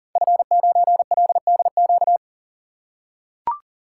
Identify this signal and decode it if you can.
Morse code, heard as a keyed tone, about 41 wpm, 700 Hz. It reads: F9LDQ
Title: Code Groups - 5 Letter:Numbers - 40wpm